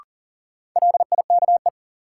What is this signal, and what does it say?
Morse code, heard as a keyed tone, about 38 wpm, 700 Hz. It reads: LIKE
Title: Sentences from Top 100 Words - New Word Drill - 40wpm